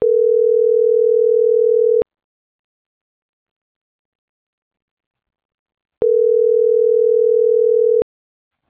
Ringback tone (North America), 440 + 480 Hz.